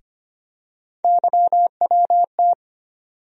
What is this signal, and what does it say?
Morse code, heard as a keyed tone, about 25 wpm, 700 Hz. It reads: YWT
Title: USA Call Sign Postfixes - 3 Letters 25wpm